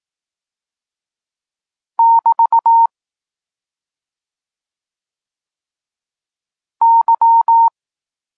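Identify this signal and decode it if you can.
Morse code, heard as a keyed tone, about 18 wpm, 920 Hz. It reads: <BT> Y
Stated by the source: Polish Morse lesson teaching the letter C, from SP9MOA Niepołomicki Klub Krótkofalowców.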